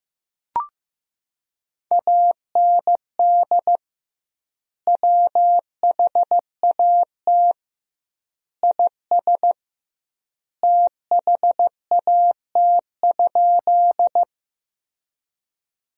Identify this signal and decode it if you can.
Morse code, heard as a keyed tone, about 15 wpm, 700 Hz. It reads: AND WHAT IS THAT?
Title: 2x Word-Spacing - Sentences from Top 100 Words - 15wpm